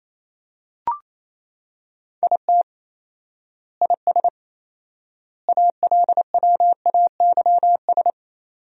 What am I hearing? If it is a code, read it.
Morse code, heard as a keyed tone, about 28 wpm, 700 Hz. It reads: IT IS ALWAYS
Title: 4x Word-Spacing - Sentences from Top 300 Words - 28wpm